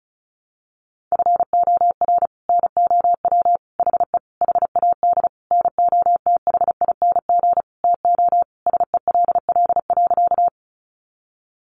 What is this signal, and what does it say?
Morse code, heard as a keyed tone, about 34 wpm, 700 Hz. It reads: FOR NOW HE HAD NOTHING TO SELL.
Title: Sentences from Top 1100 Words 35wpm